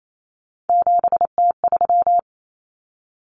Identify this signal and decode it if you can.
Morse code, heard as a keyed tone, about 28 wpm, 700 Hz. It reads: 7T3
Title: Common Cut Numbers - 3-Digit 28wpm